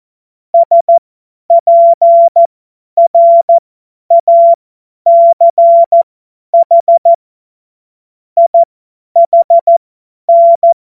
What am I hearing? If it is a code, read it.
Morse code, heard as a keyed tone, about 14 wpm, spaced as for 10 wpm, 680 Hz. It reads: SPRACH IHN
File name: Morsekurs / Witze / Witz 10. Mai 2022 50 F70